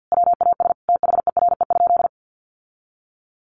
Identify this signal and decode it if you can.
Morse code, heard as a keyed tone, about 50 wpm, 700 Hz. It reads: WAS THERE?
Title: Sentences from Top 1200 Words 50wpm